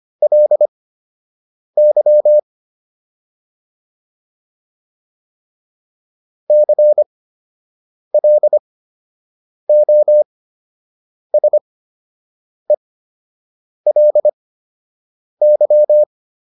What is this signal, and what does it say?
Morse code, heard as a keyed tone, about 25 wpm, spaced as for 7 wpm, 600 Hz. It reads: LY CLOSELY